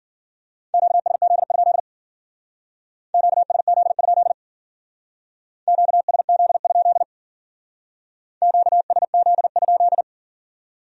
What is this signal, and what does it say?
Morse code, heard as a keyed tone, about 50 wpm, 700 Hz. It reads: QSZ? QSZ? QSZ? QSZ?
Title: Speed-Racing Q-Codes - 40wpm